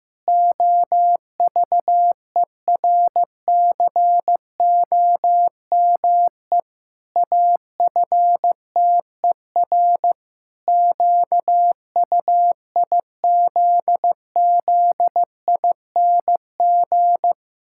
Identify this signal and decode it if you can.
Morse code, heard as a keyed tone, about 15 wpm, 700 Hz. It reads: OVERCOME AFTER QUIZZING
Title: Pangram Challenge v2 - 15wpm